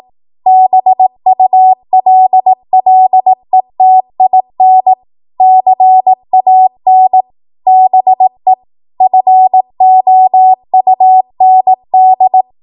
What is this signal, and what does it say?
Morse code, heard as a keyed tone, about 18 wpm, 750 Hz. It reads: BULLETIN CAN BE FOUND
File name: Morse / 18WPM